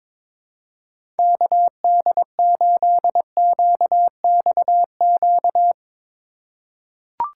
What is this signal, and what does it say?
Morse code, heard as a keyed tone, about 22 wpm, 700 Hz. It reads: KD8QXQ
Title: USA Call Signs - 4x Character Spacing - 22wpm